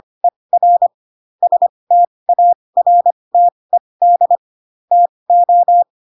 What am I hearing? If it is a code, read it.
Morse code, heard as a keyed tone, about 25 wpm, spaced as for 20 wpm, 700 Hz. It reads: ER STARTED TO